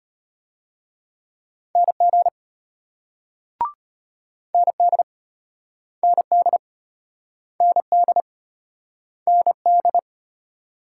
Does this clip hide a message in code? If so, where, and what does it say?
Yes: Morse code from 1.8 to 10.0 s, heard as a keyed tone, about 35 wpm, 700 Hz. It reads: NG ND ND ND ND